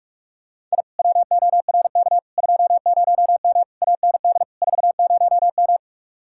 Morse code, heard as a keyed tone, about 49 wpm, 700 Hz. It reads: I WORK 20M AND 40M